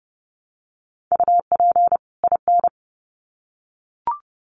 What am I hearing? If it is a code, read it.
Morse code, heard as a keyed tone, about 30 wpm, 700 Hz. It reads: UP IN
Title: Sets of 2 Words 30wpm